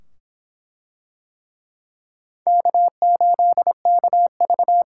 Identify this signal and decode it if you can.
Morse code, heard as a keyed tone, about 24 wpm, 700 Hz. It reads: K8KV